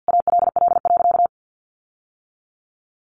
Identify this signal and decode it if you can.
Morse code, heard as a keyed tone, about 50 wpm, 700 Hz. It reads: ALL.
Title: Sentences from Top 1400 Words 50wpm